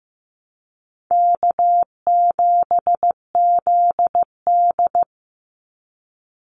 Morse code, heard as a keyed tone, about 15 wpm, 700 Hz. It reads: K7ZD
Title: Easy Call Signs - 15wpm